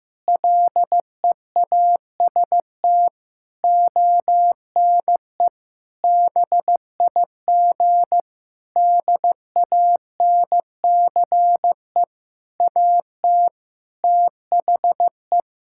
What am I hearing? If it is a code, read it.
Morse code, heard as a keyed tone, about 15 wpm, 700 Hz. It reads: LEAST ONE BIG DANCE AT THE
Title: Sentences from Top 1600 Words 15wpm